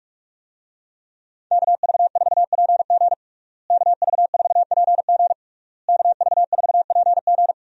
Morse code, heard as a keyed tone, about 41 wpm, 700 Hz. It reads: KV4PG KV4PG KV4PG
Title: USA Call Signs - 5 Characters - Repeated 3x - 45wpm